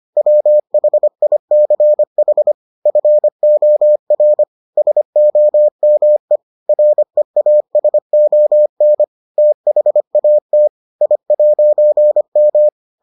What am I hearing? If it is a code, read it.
Morse code, heard as a keyed tone, about 25 wpm, 600 Hz. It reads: WHICH FOR SOME REASON THAT I'M